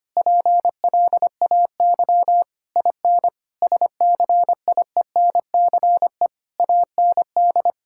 Morse code, heard as a keyed tone, about 25 wpm, 700 Hz. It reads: PLAY IN SCIENCE AND
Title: Sentences from Top 1000 Words - Review Part 6 - 25wpm